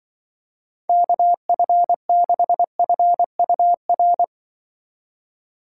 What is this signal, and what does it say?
Morse code, heard as a keyed tone, about 25 wpm, 700 Hz. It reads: KF6FUR
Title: Speed Racing - USA 2x3 Call Signs 20wpm